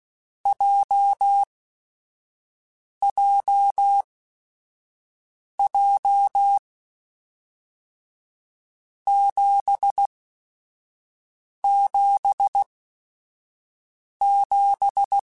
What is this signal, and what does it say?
Morse code, heard as a keyed tone, about 16 wpm, spaced as for 5 wpm, 800 Hz. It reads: JJJ 777
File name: Groups@5 / Groups@5 mp3s / JZ17@5